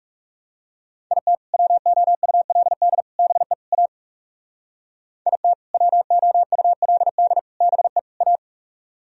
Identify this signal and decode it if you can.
Morse code, heard as a keyed tone, about 37 wpm, 700 Hz. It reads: IT WOULD BE A IT WOULD BE A
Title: Speed-Racing - Sets of 4 Words 30wpm